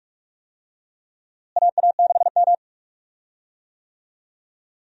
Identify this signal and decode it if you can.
Morse code, heard as a keyed tone, about 46 wpm, 700 Hz. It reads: AA6M
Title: USA Call Signs - 4 Characters - Repeated 3x - 45wpm